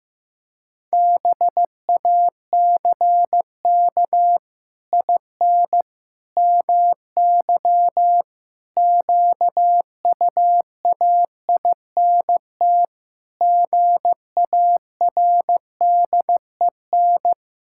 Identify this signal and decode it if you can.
Morse code, heard as a keyed tone, about 15 wpm, 700 Hz. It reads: BACK IN MY QUAINT GARDEN
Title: Pangram Challenge v2 - 15wpm